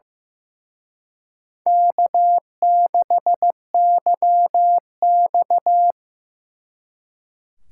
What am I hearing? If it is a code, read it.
Morse code, heard as a keyed tone, about 18 wpm, 700 Hz. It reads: K6YX
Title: Speed Racing - USA Call Signs 15wpm